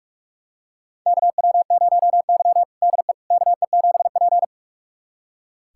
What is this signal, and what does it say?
Morse code, heard as a keyed tone, about 42 wpm, 700 Hz. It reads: KW0Y DE KE7P
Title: USA Callsign DE Callsign - 4 Characters - 45wpm